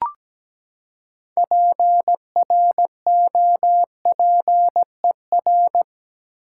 Morse code, heard as a keyed tone, about 17 wpm, 700 Hz. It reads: PROPER
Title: Six Letter Words - 17wpm